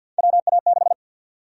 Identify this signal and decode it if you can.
Morse code, heard as a keyed tone, about 52 wpm, 700 Hz. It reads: WA6
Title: USA Call Sign Prefixes - Repeated 2x - 50wpm